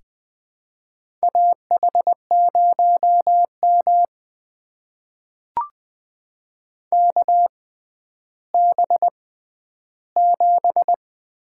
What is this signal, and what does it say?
Morse code, heard as a keyed tone, about 20 wpm, spaced as for 11 wpm, 700 Hz. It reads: AH0M KB7
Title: USA Call Signs - 6x Character Spacing - 20wpm